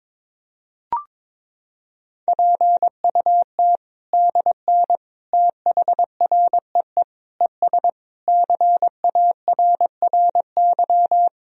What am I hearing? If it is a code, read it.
Morse code, heard as a keyed tone, about 22 wpm, 700 Hz. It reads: PUT DN THREE ES CARRY
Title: Sentences from Top 400 Words and Abbreviations 22wpm